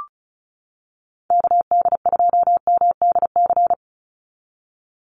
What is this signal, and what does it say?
Morse code, heard as a keyed tone, about 34 wpm, 700 Hz. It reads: KD2MDC
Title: USA Call Signs 35wpm